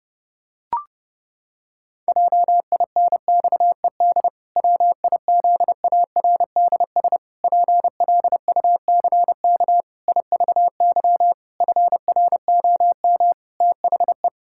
Morse code, heard as a keyed tone, about 30 wpm, 700 Hz. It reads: JINXED WIZARDS PLUCK IVY FROM THE
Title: Pangram Challenge v2 - 30wpm